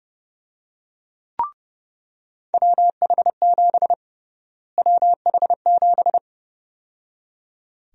Morse code, heard as a keyed tone, about 30 wpm, 700 Hz. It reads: WH7 WH7
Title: USA Call Sign Prefixes - Repeated 2x - 30wpm